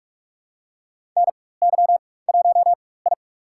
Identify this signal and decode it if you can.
Morse code, heard as a keyed tone, about 46 wpm, spaced as for 21 wpm, 700 Hz. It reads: NY1I
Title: USA Call Signs - 4 Characters - 4x Character Spacing - 45wpm